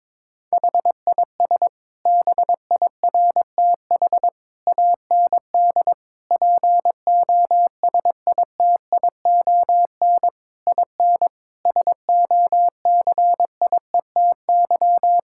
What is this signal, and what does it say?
Morse code, heard as a keyed tone, about 22 wpm, 700 Hz. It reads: HIS BIRTH AND POSITION IN SOCIETY
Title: Sentences from Top 1000 Words - Review Part 5 - 22wpm